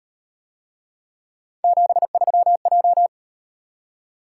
Morse code, heard as a keyed tone, about 38 wpm, 700 Hz. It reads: 73J